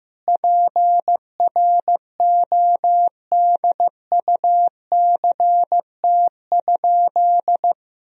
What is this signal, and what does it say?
Morse code, heard as a keyed tone, about 15 wpm, 700 Hz. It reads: PRODUCT?
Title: Sentences from Top 1000 Words - Review Part 2 - 15wpm